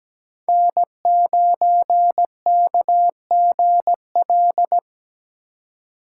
Morse code, heard as a keyed tone, about 17 wpm, 700 Hz. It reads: N9KGL
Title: USA Call Signs - 5 Characters - 4x Character Spacing - 17wpm